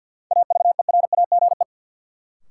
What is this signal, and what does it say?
Morse code, heard as a keyed tone, about 53 wpm, 700 Hz. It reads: AVERAGE
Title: Sentences from Top 1400 Words - New Word Drill Alt - 50wpm